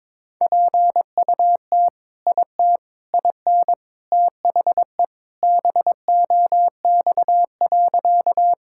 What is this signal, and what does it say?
Morse code, heard as a keyed tone, about 22 wpm, 700 Hz. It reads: PUT IT IN THE BOX.
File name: practice / 22wpm / Sentences with 3-Word Phrases - 22wpm